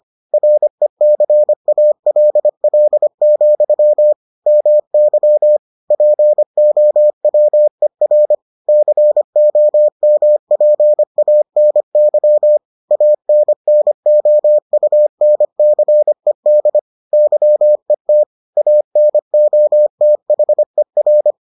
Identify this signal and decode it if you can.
Morse code, heard as a keyed tone, about 25 wpm, 600 Hz. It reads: RECALL, MY POWER COMPANY ANNOUNCED YET ANOTHER